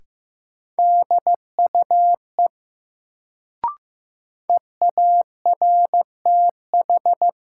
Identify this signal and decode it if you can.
Morse code, heard as a keyed tone, about 15 wpm, 700 Hz. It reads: DUE EARTH
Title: Sets of 5 Words Encore - New Word Drill - 15wpm